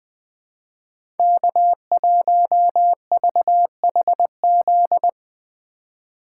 Morse code, heard as a keyed tone, about 20 wpm, 700 Hz. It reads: K1VHZ
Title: USA Call Signs - 4x Character Spacing - 20wpm